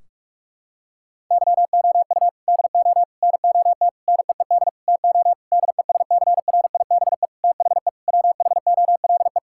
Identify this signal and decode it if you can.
Morse code, heard as a keyed tone, about 46 wpm, 700 Hz. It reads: YOU DO NOT NEED TO DESCRIBE THE WHOLE
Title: Sentences from Top 1600 Words 45wpm